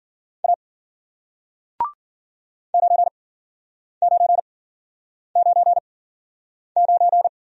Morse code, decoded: A 9999

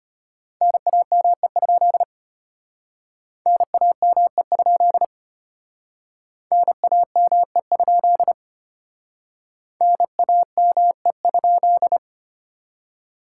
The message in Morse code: NAME? NAME? NAME? NAME?

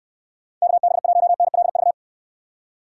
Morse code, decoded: KB1NLV

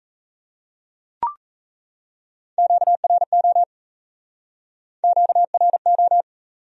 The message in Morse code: QRO QRO

